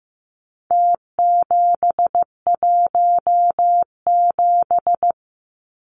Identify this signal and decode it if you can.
Morse code, heard as a keyed tone, about 15 wpm, 700 Hz. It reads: T717
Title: Numbers - Contest Style - 4-Digit 15wpm